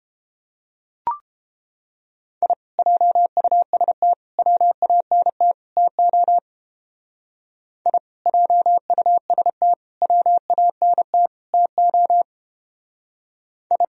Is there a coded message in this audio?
Yes: Morse code from 2.4 to 13.8 s, heard as a keyed tone, about 29 wpm, 700 Hz. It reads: I JUST WANT TO I JUST WANT TO I